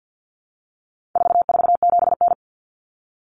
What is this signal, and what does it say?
Morse code, heard as a keyed tone, about 50 wpm, 700 Hz. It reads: 447N
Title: Common Cut Numbers - 4-Digit 50wpm